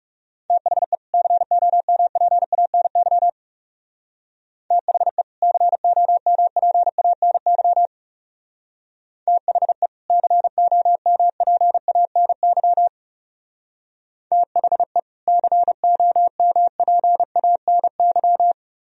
Morse code, heard as a keyed tone, about 39 wpm, 700 Hz. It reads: THE COMPANY THE COMPANY THE COMPANY THE COMPANY